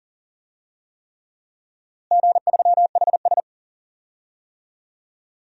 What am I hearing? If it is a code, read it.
Morse code, heard as a keyed tone, about 37 wpm, 700 Hz. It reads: G3HS
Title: DXCC-100 Call Signs - 4 Characters - Repeated 3x - 40wpm